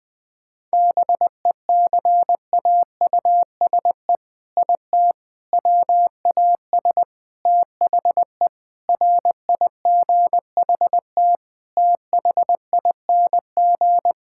Morse code, heard as a keyed tone, about 21 wpm, 700 Hz. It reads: BECAUSE IT WAS THE RIGHT THING
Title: Sets of 8 Words - 20wpm